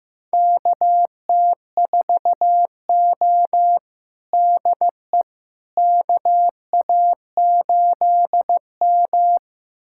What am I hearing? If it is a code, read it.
Morse code, heard as a keyed tone, about 15 wpm, 700 Hz. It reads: KT4O DE KA8M
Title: USA Callsign DE Callsign - 4 Characters - 15wpm